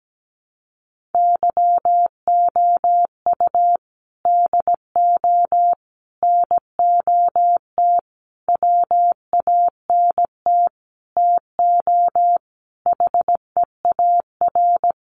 Morse code, heard as a keyed tone, about 17 wpm, 700 Hz. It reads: YOU DO NOT WANT TO HEAR
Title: Sentences from Top 1400 Words 17wpm